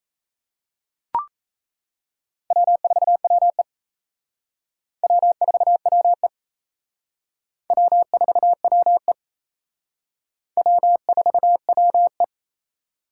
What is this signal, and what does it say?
Morse code, heard as a keyed tone, about 38 wpm, 700 Hz. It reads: W4WE W4WE W4WE W4WE